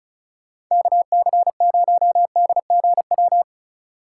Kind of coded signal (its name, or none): Morse code